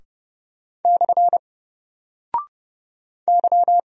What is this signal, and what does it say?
Morse code, heard as a keyed tone, about 29 wpm, 700 Hz. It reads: / Y
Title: Single Character (Letter:Number:Punctuation:Prosign) 30wpm